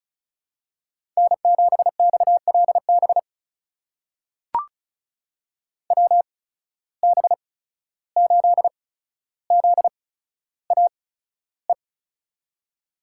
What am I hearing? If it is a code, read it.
Morse code, heard as a keyed tone, about 36 wpm, spaced as for 14 wpm, 700 Hz. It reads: N7XLB WB8ZAE
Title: USA Call Signs - 8x Character Spacing - 35wpm